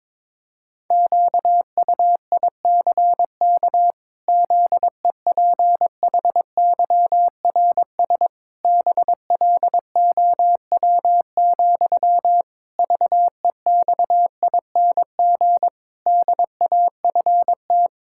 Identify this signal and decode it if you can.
Morse code, heard as a keyed tone, about 22 wpm, 700 Hz. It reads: QUICK ZEPHYRS BLOW, VEXING DAFT
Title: Pangram Challenge v2 - 22wpm